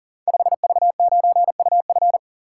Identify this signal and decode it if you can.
Morse code, heard as a keyed tone, about 40 wpm, 690 Hz. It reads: 5V9UF
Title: Code Groups - 5 Letter:Numbers - 40wpm